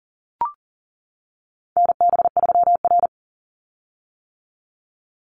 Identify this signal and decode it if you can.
Morse code, heard as a keyed tone, about 40 wpm, 700 Hz. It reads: NB3R
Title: Call Signs - 40wpm